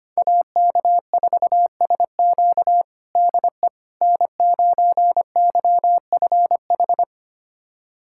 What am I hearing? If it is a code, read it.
Morse code, heard as a keyed tone, about 25 wpm, 700 Hz. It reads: AK4SQ DE N9YFH
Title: USA Callsign DE Callsign - 5 Characters - 25wpm